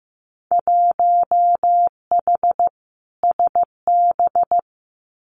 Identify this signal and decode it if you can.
Morse code, heard as a keyed tone, about 15 wpm, 700 Hz. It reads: 1H SB
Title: Winter Field Day Exchanges 15wpm